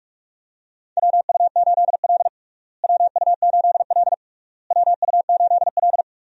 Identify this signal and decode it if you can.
Morse code, heard as a keyed tone, about 46 wpm, 700 Hz. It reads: WU8L WU8L WU8L
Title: USA Call Signs - 4 Characters - Repeated 3x - 45wpm